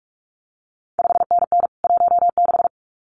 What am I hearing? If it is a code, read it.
Morse code, heard as a keyed tone, about 46 wpm, 700 Hz. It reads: 5NN 16
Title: CQ WW DX Contest Exchange - 45wpm